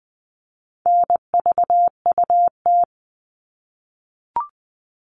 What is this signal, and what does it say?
Morse code, heard as a keyed tone, about 20 wpm, 700 Hz. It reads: NVUT
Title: All Cut Numbers - 4-Digit 20wpm